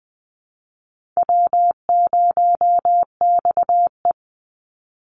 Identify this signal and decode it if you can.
Morse code, heard as a keyed tone, about 20 wpm, 700 Hz. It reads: W0XE
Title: Easy Call Signs Repeated 3x - 20wpm